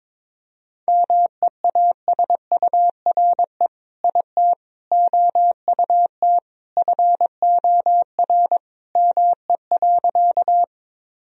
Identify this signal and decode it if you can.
Morse code, heard as a keyed tone, about 22 wpm, 700 Hz. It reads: MEASURE IT OUT FOR ME.
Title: Sentences from Top 1000 Words - Review Part 2 - 22wpm